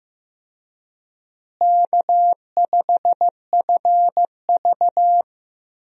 Morse code, heard as a keyed tone, about 16 wpm, 700 Hz. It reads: K5FV
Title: USA Call Signs - 4 Characters - Repeated 3x - 15wpm